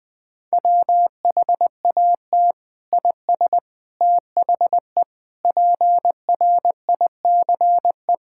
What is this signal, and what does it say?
Morse code, heard as a keyed tone, about 20 wpm, 700 Hz. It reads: WHAT IS THE PRICE
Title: Sentences from Top 1000 Words - Review Part 5 - 20wpm